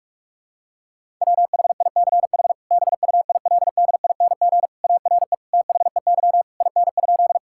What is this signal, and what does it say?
Morse code, heard as a keyed tone, about 46 wpm, 690 Hz. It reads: WHICH BUILDING ARE THEY IN?
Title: Sentences from Top 1000 Words - Review Part 5 - 45wpm